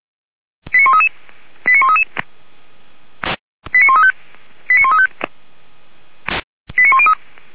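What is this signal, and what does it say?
Five-tone selective call (ZVEI), heard as quick runs of five tones, tones of about 70 ms. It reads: 89133 89133 89135 89135 89103
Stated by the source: ZVEI five-tone selective call received by SDR.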